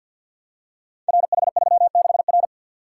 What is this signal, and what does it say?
Morse code, heard as a keyed tone, about 51 wpm, 700 Hz. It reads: RH36R
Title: Code Groups - 5 Letter:Numbers - 50wpm